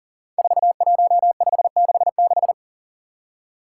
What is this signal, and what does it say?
Morse code, heard as a keyed tone, about 40 wpm, 700 Hz. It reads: 41566
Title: Five Challenging Letter-Number Combinations 40wpm